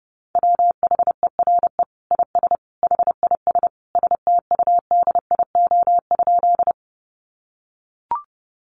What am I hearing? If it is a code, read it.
Morse code, heard as a keyed tone, about 30 wpm, 700 Hz. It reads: WHERE IS HIS STUDIO?
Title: Sentences from Top 1200 Words 30wpm